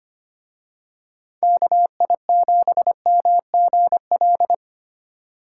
Morse code, heard as a keyed tone, about 29 wpm, 700 Hz. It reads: KI7MGL
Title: Speed Racing - USA 2x3 Call Signs 25wpm